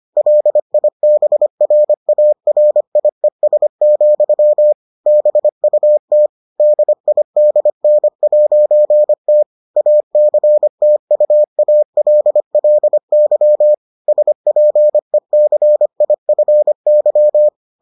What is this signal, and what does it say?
Morse code, heard as a keyed tone, about 25 wpm, 600 Hz. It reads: LIBRARIES, BUT DIDN'T ACTUALLY SPECIFY